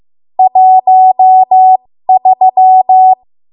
Morse code, heard as a keyed tone, about 15 wpm, spaced as for 13 wpm, 750 Hz. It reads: 13
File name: Morse / 13WPM